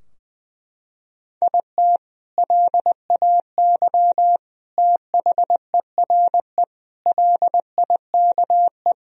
Morse code, heard as a keyed tone, about 21 wpm, 700 Hz. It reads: IT LAY THERE LIKE
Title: Sentences from Top 1000 Words - Review Part 3 - 20wpm